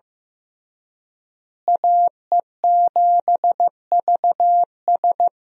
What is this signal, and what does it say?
Morse code, heard as a keyed tone, about 15 wpm, 700 Hz. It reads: AE7VS
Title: USA Call Signs - 5 Characters - Repeated 2x - 15wpm